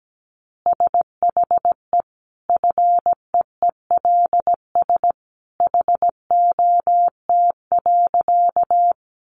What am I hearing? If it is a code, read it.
Morse code, heard as a keyed tone, about 17 wpm, 700 Hz. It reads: SHE FEELS HOT.
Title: Additional Easy Phrases 17wpm